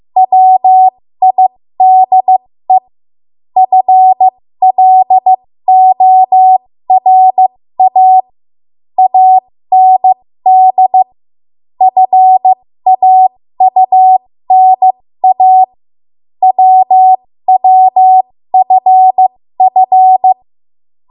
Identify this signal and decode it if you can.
Morse code, heard as a keyed tone, about 15 wpm, spaced as for 13 wpm, 750 Hz. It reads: WIDE FLORA AND FAUNA WWFF
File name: Morse / 13WPM